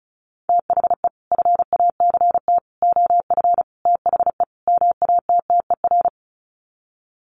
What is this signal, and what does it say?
Morse code, heard as a keyed tone, about 34 wpm, 700 Hz. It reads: THE FACT OF THE MATTER
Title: Sets of 5 Words 35wpm